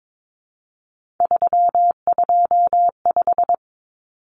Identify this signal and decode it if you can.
Morse code, heard as a keyed tone, about 22 wpm, 700 Hz. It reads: 325
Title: Numbers - 3-Digit 22wpm